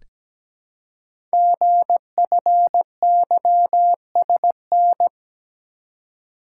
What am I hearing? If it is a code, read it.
Morse code, heard as a keyed tone, about 17 wpm, 700 Hz. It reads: GFYSN
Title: Code Groups - 5 Letter:Numbers - 17wpm